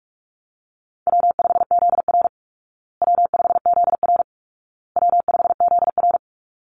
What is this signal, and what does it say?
Morse code, heard as a keyed tone, about 44 wpm, 700 Hz. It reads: W5ZR W5ZR W5ZR
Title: Easy Call Signs Repeated 3x - 45wpm